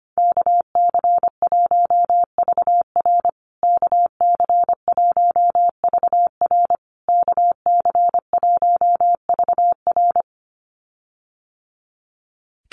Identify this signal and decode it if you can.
Morse code, heard as a keyed tone, about 25 wpm, 700 Hz. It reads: KC1VR KC1VR KC1VR
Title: USA Call Signs Repeated 3x 25wpm